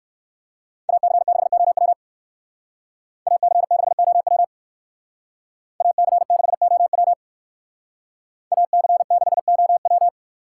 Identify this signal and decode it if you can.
Morse code, heard as a keyed tone, about 58 wpm, 700 Hz. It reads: AC6OW AC6OW AC6OW AC6OW